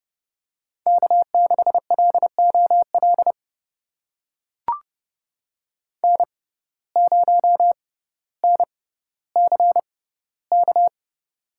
Morse code, heard as a keyed tone, about 29 wpm, spaced as for 15 wpm, 700 Hz. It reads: K6LOL N0NCK
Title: USA Call Signs - 5 Characters - 6x Character Spacing - 30wpm